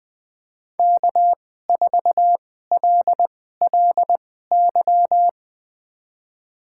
Morse code, heard as a keyed tone, about 20 wpm, spaced as for 14 wpm, 700 Hz. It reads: K4LLY
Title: USA Call Signs - 5 Characters - 2x Character Spacing - 20wpm